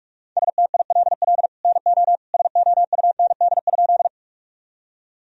Morse code, heard as a keyed tone, about 46 wpm, 700 Hz. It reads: STILL NO SOUND?
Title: Sentences from Top 1000 Words - Review Part 1 - 45wpm